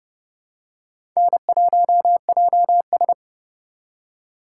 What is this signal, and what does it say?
Morse code, heard as a keyed tone, about 30 wpm, 700 Hz. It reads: N1JS
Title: USA Call Signs - 4 Characters - 2x Character Spacing - 30wpm